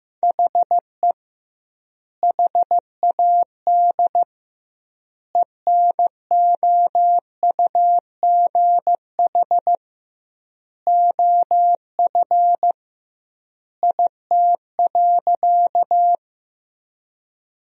Morse code, heard as a keyed tone, about 15 wpm, 700 Hz. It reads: HE HAD ENOUGH OF IT.
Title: 2x Word-Spacing - Sentences from Top 400 Words - 15wpm